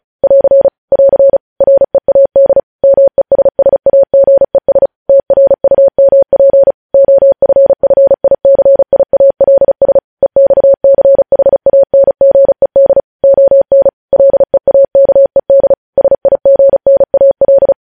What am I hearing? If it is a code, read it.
Morse code, heard as a keyed tone, about 36 wpm, 550 Hz. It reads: <AR> <AR> READ MESSAGES TRUMP OFFICIALS EXCHANGED ON LEAKED SIGNAL
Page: News Headlines in Morse Code at 35 WPM